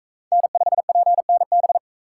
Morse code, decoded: N5PNB

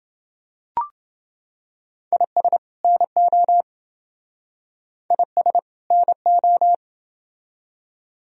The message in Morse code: IS NO IS NO

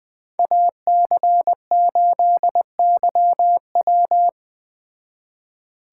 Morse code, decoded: AC8YW

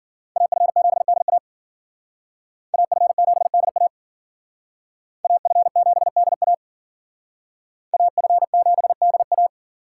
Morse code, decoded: AF7DA AF7DA AF7DA AF7DA

